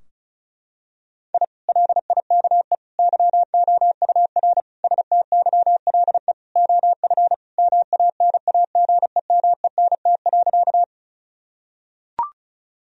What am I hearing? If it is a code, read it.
Morse code, heard as a keyed tone, about 36 wpm, 700 Hz. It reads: I LIKE YOUR STYLE OF MANAGEMENT.
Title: Sentences from Top 1000 Words - Review Part 4 - 35wpm